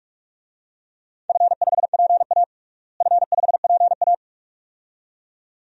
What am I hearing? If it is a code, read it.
Morse code, heard as a keyed tone, about 41 wpm, 700 Hz. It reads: F5PA F5PA
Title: DXCC-100 Call Signs - 4 Characters - Repeated 2x - 45wpm